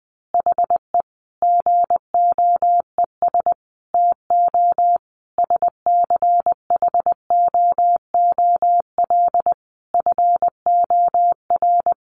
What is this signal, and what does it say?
Morse code, heard as a keyed tone, about 20 wpm, 700 Hz. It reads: HE GOES TO SCHOOL FOR
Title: Additional Easy Phrases 20wpm